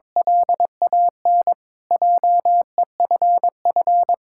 Morse code, heard as a keyed tone, about 22 wpm, 700 Hz. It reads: LAN JEFF
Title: K1USN SST QSO Practice - 22wpm